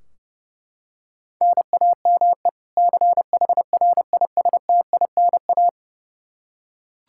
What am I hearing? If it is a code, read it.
Morse code, heard as a keyed tone, about 30 wpm, 700 Hz. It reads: NAME CHRISTINA
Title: Practice QSO - Exchange 1 - Beginning - 30wpm